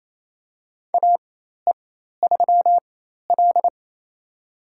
Morse code, heard as a keyed tone, about 28 wpm, spaced as for 13 wpm, 700 Hz. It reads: AE3L
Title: USA Call Signs - 4x Character Spacing - 28wpm